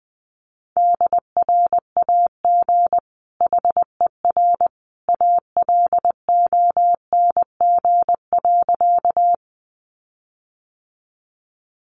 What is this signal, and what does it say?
Morse code, heard as a keyed tone, about 20 wpm, 700 Hz. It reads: DRAG HER ALONG.
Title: Sentences from Top 1500 Words 20wpm